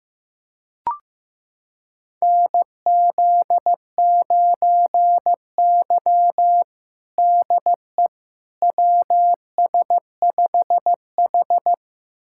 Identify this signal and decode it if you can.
Morse code, heard as a keyed tone, about 15 wpm, 700 Hz. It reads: NZ9Y DE WS5H
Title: USA Callsign DE Callsign - 4 Characters - 15wpm